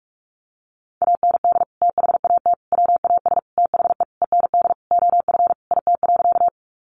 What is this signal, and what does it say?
Morse code, heard as a keyed tone, about 44 wpm, 700 Hz. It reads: AND THAT WAS THE END OF IT.
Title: Sentences from Top 200 Words 45wpm